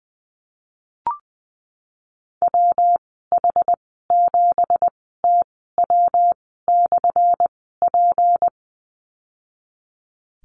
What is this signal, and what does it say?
Morse code, heard as a keyed tone, about 19 wpm, spaced as for 14 wpm, 700 Hz. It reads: WH7TW/P
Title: 179 Lesson 29g - Slash - Call Signs 20wpm